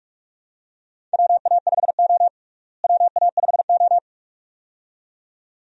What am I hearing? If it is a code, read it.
Morse code, heard as a keyed tone, about 46 wpm, 700 Hz. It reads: WA5O WA5O
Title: USA Call Signs - 4 Characters - Repeated 2x - 45wpm